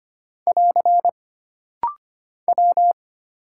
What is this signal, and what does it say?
Morse code, heard as a keyed tone, about 25 wpm, 700 Hz. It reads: <AR> W
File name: Single Character (Letter:Number:Punctuation:Prosign) - Rapid-Fire - 25wpm